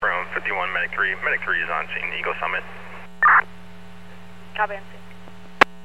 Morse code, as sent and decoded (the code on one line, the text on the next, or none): none